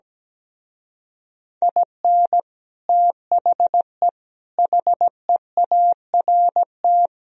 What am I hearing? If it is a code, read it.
Morse code, heard as a keyed tone, about 17 wpm, 700 Hz. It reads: IN THE HEART
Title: Speed-Racing - Sets of 5 Words 17wpm